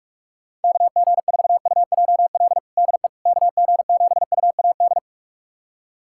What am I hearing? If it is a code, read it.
Morse code, heard as a keyed tone, about 46 wpm, 700 Hz. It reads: KG4UJL DE KG7UAD
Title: USA Callsign DE Callsign 45wpm